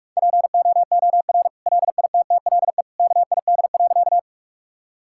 Morse code, heard as a keyed tone, about 46 wpm, 700 Hz. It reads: POOR LITTLE KID.
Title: Sentences from Top 1600 Words 45wpm